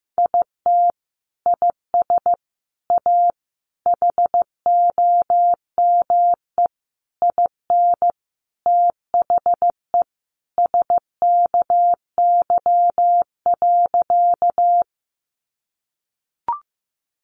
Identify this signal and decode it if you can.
Morse code, heard as a keyed tone, about 15 wpm, 700 Hz. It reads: IT IS A HOME IN THE SKY.
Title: Sentences from Top 1200 Words 15wpm